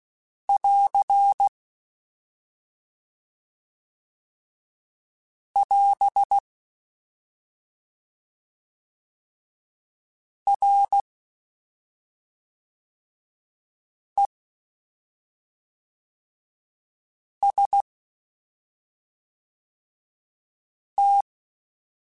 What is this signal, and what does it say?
Morse code, heard as a keyed tone, about 16 wpm, 800 Hz. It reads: <AR> <AS> R E S T
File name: Combinations / Combinations-mp3s / Combinations@5